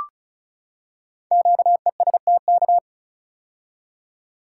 Morse code, heard as a keyed tone, about 34 wpm, 700 Hz. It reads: QESTK
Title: Code Groups - 5 Letters - 35wpm